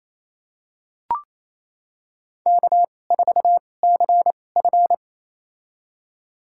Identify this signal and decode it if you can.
Morse code, heard as a keyed tone, about 28 wpm, spaced as for 20 wpm, 700 Hz. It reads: K4CF
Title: USA Call Signs - 4 Characters - 2x Character Spacing - 28wpm